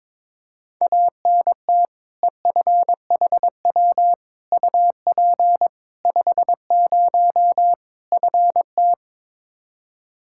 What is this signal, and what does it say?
Morse code, heard as a keyed tone, about 22 wpm, 700 Hz. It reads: ANT EFHW UP 50 FT
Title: Practice QSO - Exchange 2 - Info - 22wpm